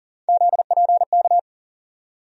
Morse code, heard as a keyed tone, about 40 wpm, 700 Hz. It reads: ZPK
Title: USA Call Sign Postfixes - 3 Letters 40wpm